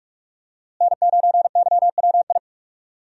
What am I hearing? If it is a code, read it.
Morse code, heard as a keyed tone, about 44 wpm, 700 Hz. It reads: N9YWI
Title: USA Call Signs - 5 Characters - Repeated 3x - 45wpm